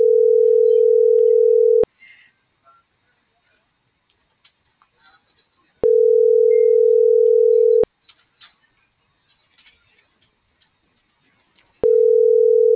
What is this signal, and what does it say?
Ringback tone (North America), 440 + 480 Hz.